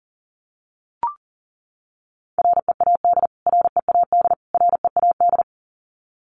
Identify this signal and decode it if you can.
Morse code, heard as a keyed tone, about 40 wpm, 700 Hz. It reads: READ READ READ
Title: Top 500 Words Repeated 40wpm